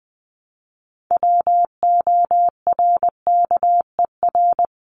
Morse code, heard as a keyed tone, about 20 wpm, 700 Hz. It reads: WORKER
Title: ES:S Suffix - Top 100 Words with Base Words 20wpm